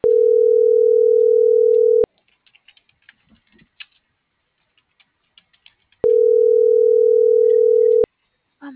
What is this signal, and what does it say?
Ringback tone (North America), 440 + 480 Hz.